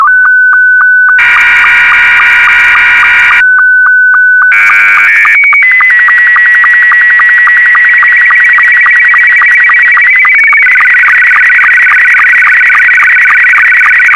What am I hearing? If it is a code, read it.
SSTV (slow-scan television), Scottie 2.
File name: sstv / S2-VK4MSL